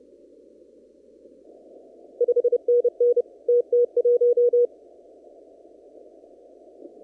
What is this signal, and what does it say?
Morse code, heard as a keyed tone, about 30 wpm, 480 Hz. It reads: 5NN TT1
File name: CW / kt5j wpx cw-op-k5pi